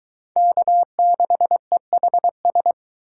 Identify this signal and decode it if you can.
Morse code, heard as a keyed tone, about 22 wpm, 700 Hz. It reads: K6EHS